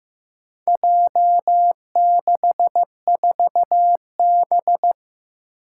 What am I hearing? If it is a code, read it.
Morse code, heard as a keyed tone, about 15 wpm, 700 Hz. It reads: J64B
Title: Four Challenging Letter-Number Combinations 15wpm